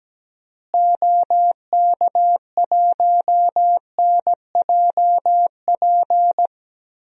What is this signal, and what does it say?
Morse code, heard as a keyed tone, about 17 wpm, 700 Hz. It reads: OK1NJP
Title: DXCC-100 Call Signs - 6 Characters - Repeated 3x - 17wpm